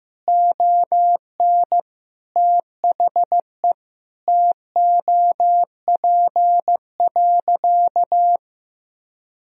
Morse code, heard as a keyed tone, about 15 wpm, 700 Hz. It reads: ON THE TOP.
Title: Sentences from Top 1000 Words - Review Part 2 - 15wpm